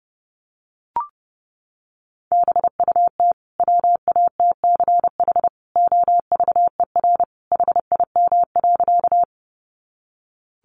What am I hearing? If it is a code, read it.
Morse code, heard as a keyed tone, about 30 wpm, 700 Hz. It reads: BUT WATCH OVER HIM.
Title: Sentences from Top 300 Words 30wpm